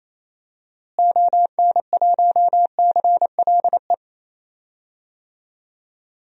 Morse code, heard as a keyed tone, about 27 wpm, 700 Hz. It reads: ON1CLE
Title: DXCC-100 Call Signs - 6 Characters - Repeated 2x - 28wpm